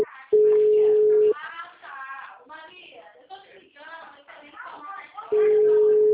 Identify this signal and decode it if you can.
Ringback tone (Europe, 425 Hz), with speech.